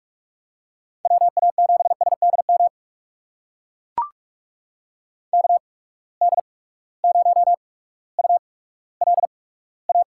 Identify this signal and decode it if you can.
Morse code, heard as a keyed tone, about 42 wpm, spaced as for 19 wpm, 700 Hz. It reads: WA7SDM KD0ULA
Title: USA Call Signs - 6 Characters - 8x Character Spacing - 45wpm